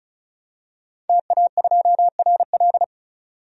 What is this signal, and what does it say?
Morse code, heard as a keyed tone, about 34 wpm, 700 Hz. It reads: TA2RL
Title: DXCC-100 Call Signs - 5 Characters - 35wpm